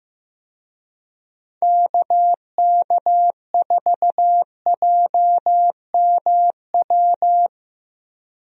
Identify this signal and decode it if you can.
Morse code, heard as a keyed tone, about 15 wpm, 700 Hz. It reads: KK4JMW
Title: USA Call Signs - 6 Characters - 15wpm